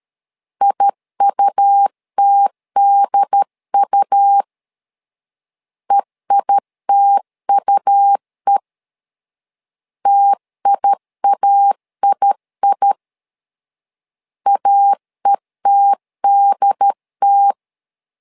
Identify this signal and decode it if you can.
Morse code, heard as a keyed tone, about 13 wpm, 800 Hz. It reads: IUTDU EITUE TIAII AETDT
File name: LEZIONI / 2 UHDN / 2 UHD 12wpm 1540